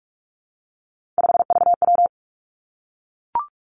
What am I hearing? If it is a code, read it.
Morse code, heard as a keyed tone, about 46 wpm, 700 Hz. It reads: 5VW
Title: Three Challenging Letter-Number Combinations 45wpm